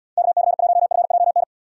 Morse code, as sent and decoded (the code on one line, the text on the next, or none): -.- -.-. .---- ..-. .--- -.
KC1FJN